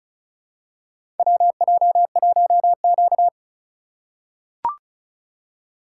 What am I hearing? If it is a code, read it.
Morse code, heard as a keyed tone, about 36 wpm, 700 Hz. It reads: WJ1Q
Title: USA Call Signs - 4 Characters - 2x Character Spacing - 35wpm